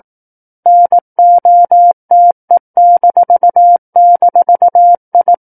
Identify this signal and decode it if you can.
Morse code, heard as a keyed tone, about 18 wpm, 700 Hz. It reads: NOTE--I
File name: 18 WPM